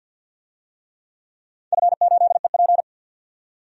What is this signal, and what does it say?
Morse code, heard as a keyed tone, about 53 wpm, 700 Hz. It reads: F8EP
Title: DXCC-100 Call Signs - 4 Characters - 50wpm